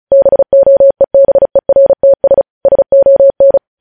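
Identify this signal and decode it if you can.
Morse code, heard as a keyed tone, about 36 wpm, 550 Hz. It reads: BOEBERTS SON
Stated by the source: News headlines sent in Morse at speed 35.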